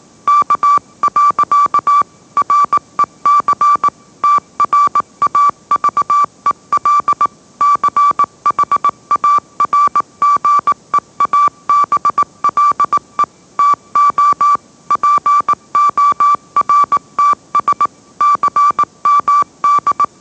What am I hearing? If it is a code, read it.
Morse code, heard as a keyed tone, about 18 wpm, 1160 Hz. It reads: K. REC TRAVEL CHARGEABLE TO PORTS CMD